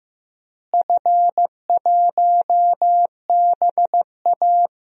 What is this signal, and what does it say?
Morse code, heard as a keyed tone, about 15 wpm, 700 Hz. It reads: F1BA
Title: DXCC-100 Call Signs - 4 Characters - Repeated 3x - 15wpm